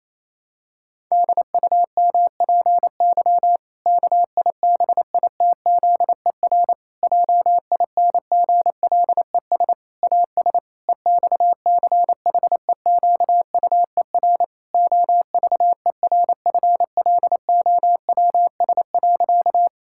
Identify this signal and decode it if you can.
Morse code, heard as a keyed tone, about 28 wpm, 700 Hz. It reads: DUMPY KIBITZER JINGLES AS EXCHEQUER OVERFLOWS.
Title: Pangram Challenge v2 - 28wpm